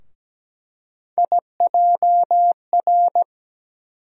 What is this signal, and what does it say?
Morse code, heard as a keyed tone, about 17 wpm, 700 Hz. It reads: IJR
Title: USA Call Sign Postfixes - 3 Letters 17wpm